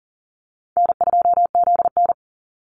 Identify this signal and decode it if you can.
Morse code, heard as a keyed tone, about 40 wpm, 700 Hz. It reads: N2ZN
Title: Easy Call Signs - 40wpm